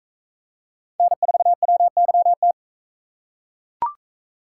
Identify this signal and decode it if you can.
Morse code, heard as a keyed tone, about 40 wpm, 700 Hz. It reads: N4WYT